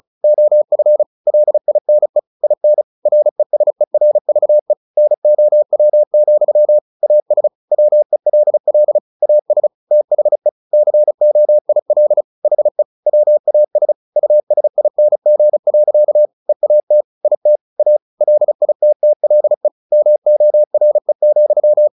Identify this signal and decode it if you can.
Morse code, heard as a keyed tone, about 34 wpm, 600 Hz. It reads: OF LINE IN RESERVE NOW, AS WELL AS THE COIL HE WAS USING. EAT IT A LITTLE MORE,
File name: the old man and the sea / the old man and the sea 1 35wpm